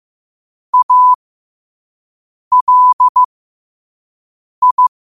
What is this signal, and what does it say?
Morse code, heard as a keyed tone, about 15 wpm, spaced as for 5 wpm, 1000 Hz. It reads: ALI